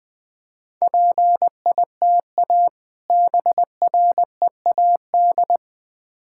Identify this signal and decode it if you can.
Morse code, heard as a keyed tone, about 21 wpm, 700 Hz. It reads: PITA BREAD
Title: Difficult Words with Hints - 20wpm